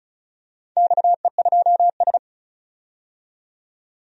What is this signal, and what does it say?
Morse code, heard as a keyed tone, about 36 wpm, 700 Hz. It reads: XE2S
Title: DXCC-100 Call Signs - 4 Characters - 35wpm